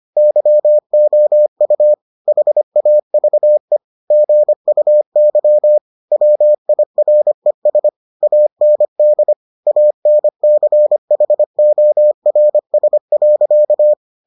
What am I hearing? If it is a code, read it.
Morse code, heard as a keyed tone, about 25 wpm, 600 Hz. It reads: YOU HAVE GUY WIRES AND ANCHORS.